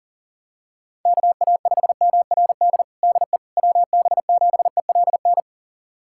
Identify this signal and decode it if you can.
Morse code, heard as a keyed tone, about 40 wpm, 700 Hz. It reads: KA5MRD DE WB7ELN
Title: USA Callsign DE Callsign 40wpm